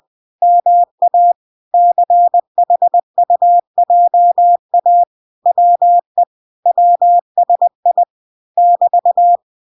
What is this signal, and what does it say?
Morse code, heard as a keyed tone, about 20 wpm, 700 Hz. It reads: MA CHUJA WE WSI <BT>